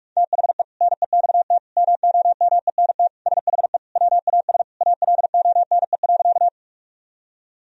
Morse code, heard as a keyed tone, about 46 wpm, 700 Hz. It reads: THE NEXT MOMENT SHE WAS ALONE.
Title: Sentences from Top 1000 Words - Review Part 1 - 45wpm